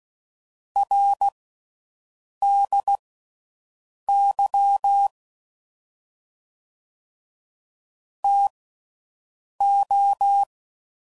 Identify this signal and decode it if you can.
Morse code, heard as a keyed tone, about 16 wpm, spaced as for 6 wpm, 800 Hz. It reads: RDY TO